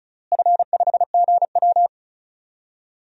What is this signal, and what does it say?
Morse code, heard as a keyed tone, about 34 wpm, 700 Hz. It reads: F5GW
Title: DXCC-100 Call Signs - 4 Characters - Repeated 3x - 35wpm